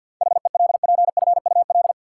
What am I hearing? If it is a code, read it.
Morse code, heard as a keyed tone, about 53 wpm, 700 Hz. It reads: HELPFUL
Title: Sentences from Top 1000 Words - Review Part 6 - New Word Drill - 50wpm